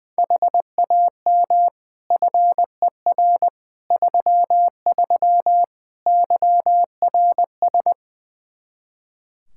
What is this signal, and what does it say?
Morse code, heard as a keyed tone, about 20 wpm, 700 Hz. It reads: HAM FER 33 YRS
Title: Practice QSO - Exchange 3 - More Info - 20wpm